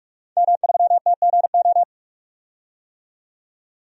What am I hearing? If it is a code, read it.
Morse code, heard as a keyed tone, about 43 wpm, 700 Hz. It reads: M3TGO
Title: DXCC-100 Call Signs - 5 Characters - 45wpm